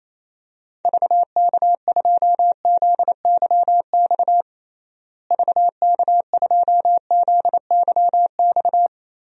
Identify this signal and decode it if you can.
Morse code, heard as a keyed tone, about 27 wpm, 700 Hz. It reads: VK2ZYX VK2ZYX
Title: DXCC-100 Call Signs - 6 Characters - Repeated 3x - 28wpm